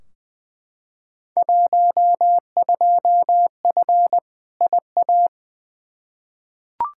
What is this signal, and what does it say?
Morse code, heard as a keyed tone, about 20 wpm, 700 Hz. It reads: 12F IA
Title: Field Day Exchanges Encore - 20wpm